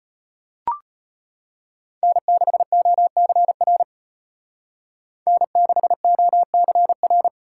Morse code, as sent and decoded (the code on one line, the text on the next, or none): -. -.... --- -.-. .-. / -. -.... --- -.-. .-.
N6OCR N6OCR